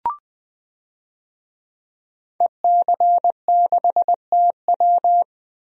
Morse code, heard as a keyed tone, about 20 wpm, 700 Hz. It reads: EC6TW
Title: DXCC-100 Call Signs - 5 Characters - Repeated 3x - 20wpm